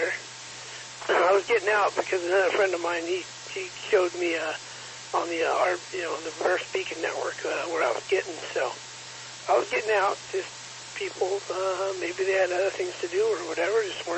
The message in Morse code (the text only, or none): none